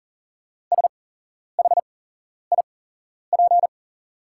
Morse code, heard as a keyed tone, about 40 wpm, spaced as for 11 wpm, 690 Hz. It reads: SHIP
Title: 8x Character Spacing - Top 500 Words - 40wpm